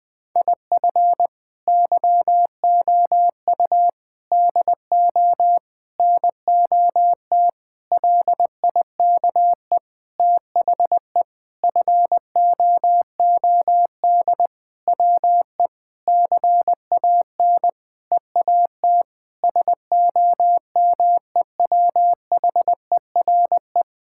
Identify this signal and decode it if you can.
Morse code, heard as a keyed tone, about 20 wpm, 700 Hz. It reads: IF YOU DO NOT LIKE THE FOOD WE CAN EAT SOMEWHERE
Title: Sentences with 5-Word Phrases - 20wpm